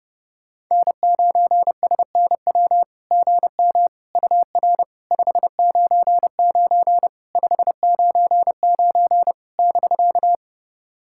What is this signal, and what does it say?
Morse code, heard as a keyed tone, about 30 wpm, 690 Hz. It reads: N9SNW GM UR 599 599 <BK>
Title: SOTA QSO Practice 30wpm